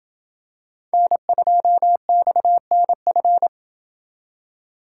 Morse code, heard as a keyed tone, about 25 wpm, 700 Hz. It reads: N2XNF
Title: Speed Racing - USA 1x3 Call Signs 20wpm